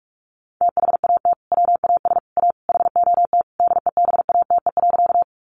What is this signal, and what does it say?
Morse code, heard as a keyed tone, about 44 wpm, 700 Hz. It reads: THAT WAS A HOT DEBATE.
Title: Sentences from Top 1100 Words 45wpm